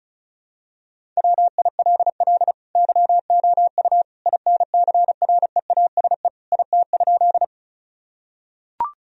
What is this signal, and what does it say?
Morse code, heard as a keyed tone, about 36 wpm, 700 Hz. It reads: WILL YOU INCREASE IT?
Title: Sentences from Top 1000 Words - Review Part 4 - 35wpm